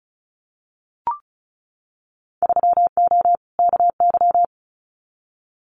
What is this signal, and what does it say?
Morse code, heard as a keyed tone, about 36 wpm, 700 Hz. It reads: 3O KY
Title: Winter Field Day Exchanges 35wpm